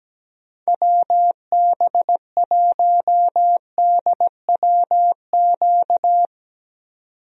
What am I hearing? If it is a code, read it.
Morse code, heard as a keyed tone, about 17 wpm, 700 Hz. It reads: WB1DWQ
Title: USA Call Signs - 6 Characters - 17wpm